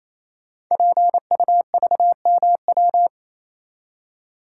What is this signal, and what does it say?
Morse code, heard as a keyed tone, about 27 wpm, 700 Hz. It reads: PUVMW
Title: Code Groups - 5 Letters - 28wpm